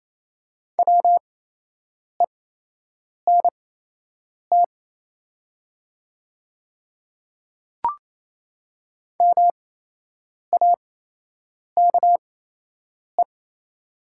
Morse code, heard as a keyed tone, about 28 wpm, spaced as for 8 wpm, 700 Hz. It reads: WENT MAKE
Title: 8x Character Spacing - Top 200 Words - 28wpm